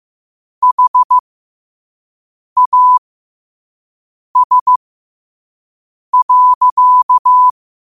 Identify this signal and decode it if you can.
Morse code, heard as a keyed tone, about 15 wpm, spaced as for 5 wpm, 1000 Hz. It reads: HAS.
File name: qotdfiles / 5wpm